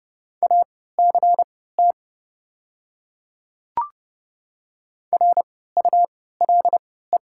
Morse code, heard as a keyed tone, about 30 wpm, spaced as for 17 wpm, 700 Hz. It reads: ACT RULE
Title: 3x Character Spacing - Top 500 Words - 30wpm